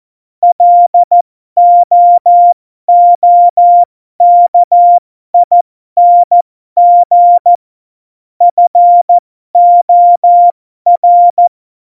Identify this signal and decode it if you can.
Morse code, heard as a keyed tone, about 14 wpm, spaced as for 12 wpm, 700 Hz. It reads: LOOKING FOR
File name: Morsekurs / Witze